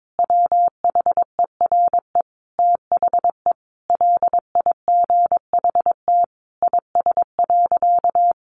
Morse code, heard as a keyed tone, about 22 wpm, 700 Hz. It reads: WHERE THE LIGHT IS.
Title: Sentences from Top 1000 Words 22wpm